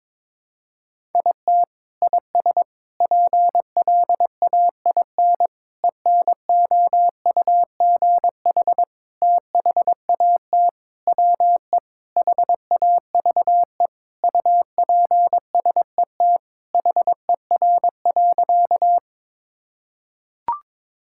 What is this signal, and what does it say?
Morse code, heard as a keyed tone, about 21 wpm, 700 Hz. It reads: IT IS PLAIN ENOUGH THAT WE HAVE UPSET HER.
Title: Sentences from Top 1000 Words - Review Part 5 - 22wpm